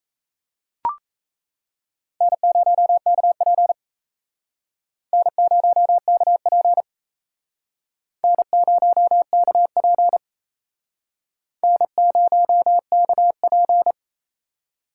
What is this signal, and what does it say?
Morse code, heard as a keyed tone, about 34 wpm, 700 Hz. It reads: N0KP N0KP N0KP N0KP
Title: Speed Racing - USA 1x2 Call Signs 28wpm